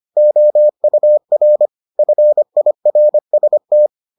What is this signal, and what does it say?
Morse code, heard as a keyed tone, about 25 wpm, 600 Hz. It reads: OUR FIRST